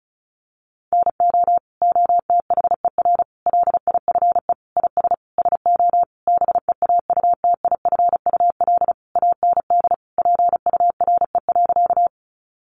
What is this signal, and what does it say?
Morse code, heard as a keyed tone, about 34 wpm, 700 Hz. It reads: NO OTHER LIFE IS SO BEAUTIFUL AND PURE.
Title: Sentences from Top 1500 Words 35wpm